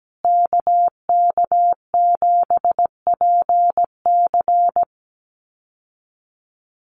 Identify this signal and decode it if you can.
Morse code, heard as a keyed tone, about 17 wpm, 700 Hz. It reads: KK7PC
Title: USA Call Signs Repeated 3x 17wpm